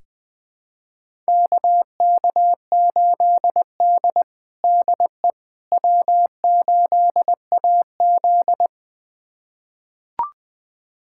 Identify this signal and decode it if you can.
Morse code, heard as a keyed tone, about 21 wpm, 700 Hz. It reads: KK8D DE W8AZ
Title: USA Callsign DE Callsign - 4 Characters - 20wpm